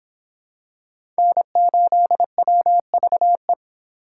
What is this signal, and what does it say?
Morse code, heard as a keyed tone, about 26 wpm, 700 Hz. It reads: N8WVE